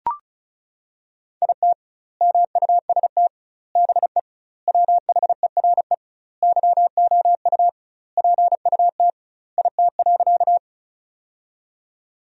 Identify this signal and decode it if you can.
Morse code, heard as a keyed tone, about 34 wpm, 700 Hz. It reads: IT MUST BE WHERE YOU PUT IT.
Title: 2x Word-Spacing - Sentences from Top 200 Words - 35wpm